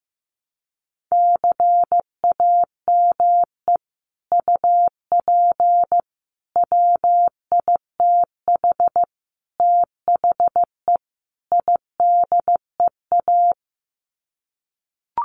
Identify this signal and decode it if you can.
Morse code, heard as a keyed tone, about 15 wpm, 700 Hz. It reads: CAME UP WITH THE IDEA
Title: Sets of 5 Words Encore2 15wpm